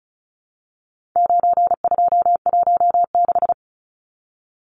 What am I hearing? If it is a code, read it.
Morse code, heard as a keyed tone, about 34 wpm, 700 Hz. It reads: 9216
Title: Numbers - 4-Digit 35wpm